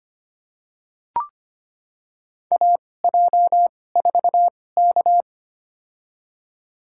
Morse code, heard as a keyed tone, about 25 wpm, spaced as for 18 wpm, 700 Hz. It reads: AJ4K
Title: USA Call Signs - 4 Characters - 2x Character Spacing - 25wpm